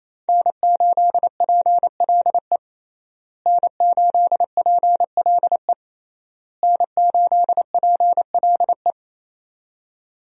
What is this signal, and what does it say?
Morse code, heard as a keyed tone, about 28 wpm, 700 Hz. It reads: N8PLE N8PLE N8PLE
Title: USA Call Signs - 5 Characters - Repeated 3x - 28wpm